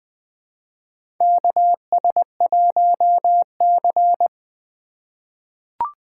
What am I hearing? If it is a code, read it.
Morse code, heard as a keyed tone, about 20 wpm, 700 Hz. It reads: KS1C
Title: USA Call Signs - 4 Characters - 4x Character Spacing - 20wpm